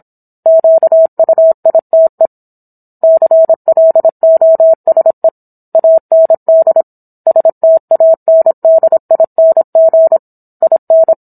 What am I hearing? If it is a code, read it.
Morse code, heard as a keyed tone, about 27 wpm, 650 Hz. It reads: QUITE CLOSE AND STANDING IN